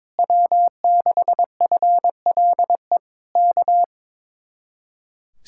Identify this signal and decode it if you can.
Morse code, heard as a keyed tone, about 21 wpm, 700 Hz. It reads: W6FLE K
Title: Speed-Racing Common QSO Phrases - 22wpm